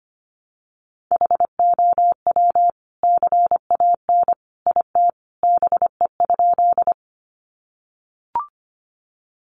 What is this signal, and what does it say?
Morse code, heard as a keyed tone, about 25 wpm, 700 Hz. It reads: HOW CAN IT BE?
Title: Sentences from Top 100 Words 25wpm